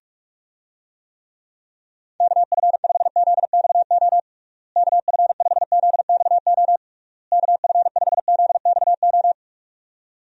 Morse code, heard as a keyed tone, about 46 wpm, 700 Hz. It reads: KF5ZXO KF5ZXO KF5ZXO
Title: USA Call Signs - 6 Characters - Repeated 3x - 45wpm